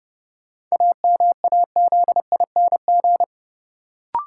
Morse code, heard as a keyed tone, about 30 wpm, 700 Hz. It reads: AMAZING
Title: Sentences from Top 1500 Words - New Word Drill - 30wpm